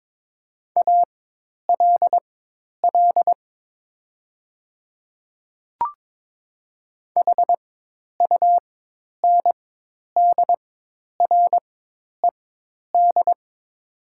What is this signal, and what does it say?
Morse code, heard as a keyed tone, about 23 wpm, spaced as for 10 wpm, 700 Hz. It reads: ALL HUNDRED
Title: 4x Character Spacing - Top 500 Words - 22wpm